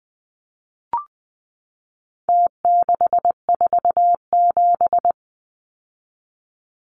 Morse code, heard as a keyed tone, about 20 wpm, 700 Hz. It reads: T647
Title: Numbers - Contest Style - 4-Digit 20wpm